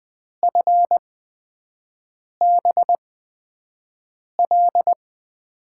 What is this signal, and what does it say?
Morse code, heard as a keyed tone, about 20 wpm, spaced as for 5 wpm, 700 Hz. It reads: FBL
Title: USA Call Signs - 6 Characters - 8x Character Spacing - 20wpm